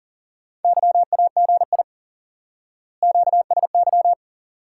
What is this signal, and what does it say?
Morse code, heard as a keyed tone, about 40 wpm, 700 Hz. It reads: YAGI QSY
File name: IQE Course 3 - Lesson 89 - YAGI - Two Elements - 40wpm